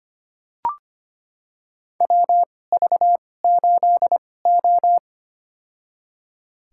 Morse code, heard as a keyed tone, about 25 wpm, spaced as for 18 wpm, 700 Hz. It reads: WV8O
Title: USA Call Signs - 4 Characters - 2x Character Spacing - 25wpm